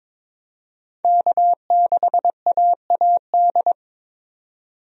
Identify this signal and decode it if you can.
Morse code, heard as a keyed tone, about 22 wpm, 700 Hz. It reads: K6AAD
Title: USA Call Signs - 5 Characters - 22wpm